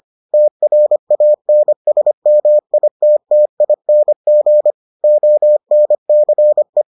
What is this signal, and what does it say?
Morse code, heard as a keyed tone, about 25 wpm, 600 Hz. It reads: TRANSMITTING ONCE